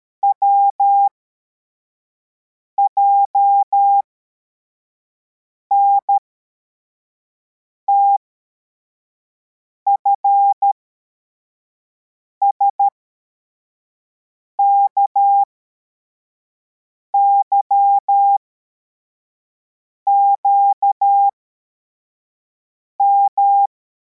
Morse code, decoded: WJNTF SKYQM